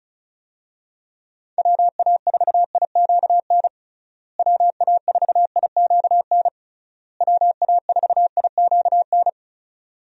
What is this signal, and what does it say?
Morse code, heard as a keyed tone, about 34 wpm, 700 Hz. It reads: WA4IQN WA4IQN WA4IQN
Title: USA Call Signs - 6 Characters - Repeated 3x - 35wpm